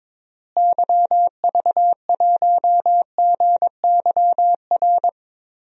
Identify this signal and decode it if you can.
Morse code, heard as a keyed tone, about 22 wpm, 700 Hz. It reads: YV1GYR
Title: DXCC-100 Call Signs - 6 Characters - Repeated 2x - 22wpm